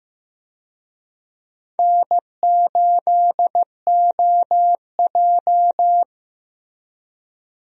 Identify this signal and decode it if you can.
Morse code, heard as a keyed tone, about 15 wpm, 700 Hz. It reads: N8OJ
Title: USA Call Signs - 4 Characters - Repeated 2x - 15wpm